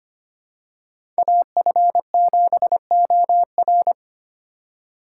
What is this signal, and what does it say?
Morse code, heard as a keyed tone, about 25 wpm, 700 Hz. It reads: AF7OR
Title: USA Call Signs - 4x Character Spacing - 25wpm